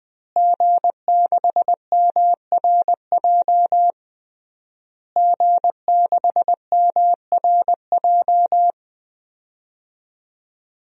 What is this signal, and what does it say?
Morse code, heard as a keyed tone, about 20 wpm, 700 Hz. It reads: G6MRJ G6MRJ
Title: DXCC-100 Call Signs - 5 Characters - Repeated 3x - 20wpm